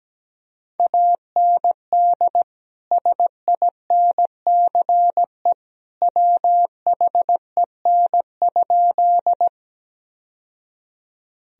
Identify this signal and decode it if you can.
Morse code, heard as a keyed tone, about 17 wpm, 700 Hz. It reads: AND SINCE WHEN?
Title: Sentences from Top 1000 Words - Review Part 1 - 17wpm